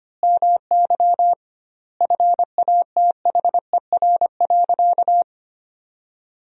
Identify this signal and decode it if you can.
Morse code, heard as a keyed tone, about 25 wpm, 700 Hz. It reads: MY FATHER.
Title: 2x Word-Spacing - Sentences from Top 500 Words - 25wpm